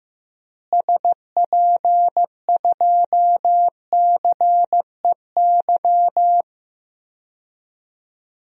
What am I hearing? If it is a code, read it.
Morse code, heard as a keyed tone, about 16 wpm, 700 Hz. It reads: SP2CEY
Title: DXCC-100 Call Signs - 6 Characters - Repeated 3x - 15wpm